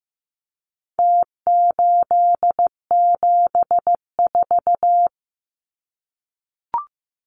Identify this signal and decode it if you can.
Morse code, heard as a keyed tone, about 15 wpm, 700 Hz. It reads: T874
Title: Numbers - Contest Style - 4-Digit 15wpm